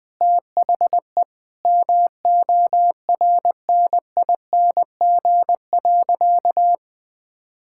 Morse code, heard as a keyed tone, about 20 wpm, 700 Hz. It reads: THE MORNING.
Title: Sentences with 3-Word Phrases - Spoken Beginning - 20wpm